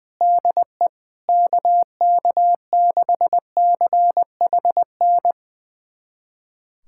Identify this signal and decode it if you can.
Morse code, heard as a keyed tone, about 20 wpm, 700 Hz. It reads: DE KK6CHN
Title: USA Callsign DE Callsign 20wpm